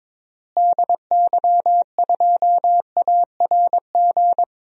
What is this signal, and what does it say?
Morse code, heard as a keyed tone, about 21 wpm, 700 Hz. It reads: DY2ARG
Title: DXCC-100 Call Signs - 6 Characters - Repeated 3x - 22wpm